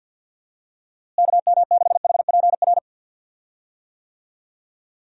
Morse code, heard as a keyed tone, about 52 wpm, 700 Hz. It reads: KM6HPR
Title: USA Call Signs - 6 Characters - Repeated 3x - 50wpm